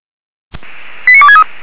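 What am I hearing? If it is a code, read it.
Five-tone selective call (ZVEI), heard as quick runs of five tones, tones of about 70 ms. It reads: 89263
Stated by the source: ZVEI five-tone selective call 89460 received on SDR, 464 channel.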